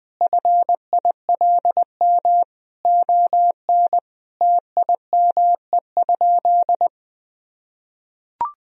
Morse code, heard as a keyed tone, about 19 wpm, 700 Hz. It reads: FILM ON TIME?
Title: Sentences from Top 1000 Words - Review Part 3 - 20wpm